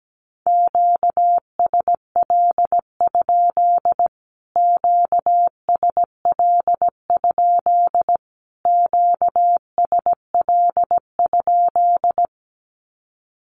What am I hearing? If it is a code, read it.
Morse code, heard as a keyed tone, about 17 wpm, 700 Hz. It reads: QSL? QSL? QSL?
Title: Q-Codes Repeated 3x 17wpm